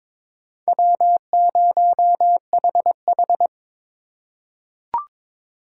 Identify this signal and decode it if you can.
Morse code, heard as a keyed tone, about 22 wpm, 700 Hz. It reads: W0HH
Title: USA Call Signs - 4 Characters - 3x Character Spacing - 22wpm